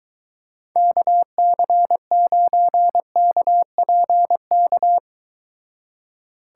Morse code, heard as a keyed tone, about 23 wpm, 700 Hz. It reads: KC9KPK